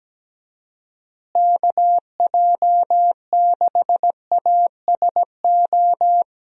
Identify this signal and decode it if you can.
Morse code, heard as a keyed tone, about 17 wpm, 700 Hz. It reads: KJ6ASO
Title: USA Call Signs - 6 Characters - Repeated 3x - 17wpm